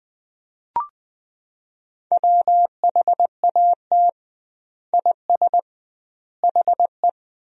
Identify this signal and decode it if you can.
Morse code, heard as a keyed tone, about 20 wpm, 700 Hz. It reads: WHAT IS HE
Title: 2x Word-Spacing - Sentences from Top 100 Words - 20wpm